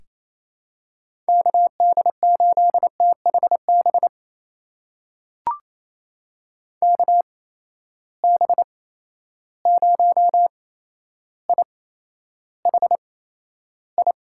Morse code, heard as a keyed tone, about 28 wpm, spaced as for 12 wpm, 700 Hz. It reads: KD8THB KB0IHI
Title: USA Call Signs - 6 Characters - 8x Character Spacing - 28wpm